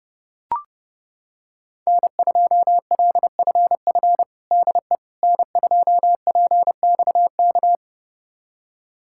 Morse code, heard as a keyed tone, about 30 wpm, 700 Hz. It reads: N2LFF DE N2PXK
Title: USA Callsign DE Callsign - 5 Characters - 30wpm